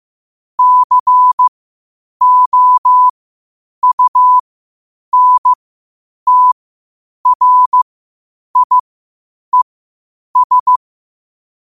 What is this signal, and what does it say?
Morse code, heard as a keyed tone, about 15 wpm, spaced as for 8 wpm, 1000 Hz. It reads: COUNTRIES
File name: qotdfiles / 8wpm